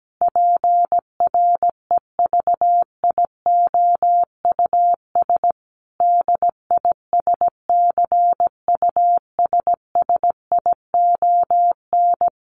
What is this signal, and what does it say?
Morse code, heard as a keyed tone, about 17 wpm, 700 Hz. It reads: PREVIOUS DISCUSSION
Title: Sentences from Top 1500 Words 17wpm